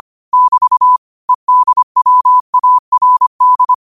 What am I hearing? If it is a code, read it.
Morse code, heard as a keyed tone, about 25 wpm, 1000 Hz. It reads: <BT> EDWARD